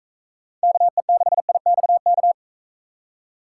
Morse code, heard as a keyed tone, about 43 wpm, 700 Hz. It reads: KE6IXK